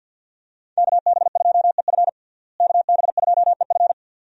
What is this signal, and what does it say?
Morse code, heard as a keyed tone, about 52 wpm, 700 Hz. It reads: KB2EF KB2EF
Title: USA Call Signs - 5 Characters - Repeated 2x - 50wpm